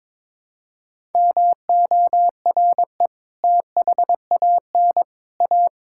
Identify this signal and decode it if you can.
Morse code, heard as a keyed tone, about 23 wpm, 700 Hz. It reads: MORE THAN A
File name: practice / 22wpm / Sets of 6 Words - 22wpm